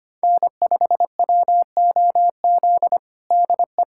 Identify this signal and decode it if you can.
Morse code, heard as a keyed tone, about 25 wpm, 700 Hz. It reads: N5WOZ DE
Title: USA Callsign DE Callsign - 5 Characters - 25wpm